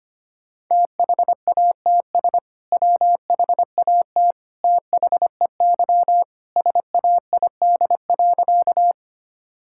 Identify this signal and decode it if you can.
Morse code, heard as a keyed tone, about 25 wpm, 700 Hz. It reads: THATS WHAT THEY SAID.
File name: practice / 25wpm / Sentences with 2-Word Phrases - 25wpm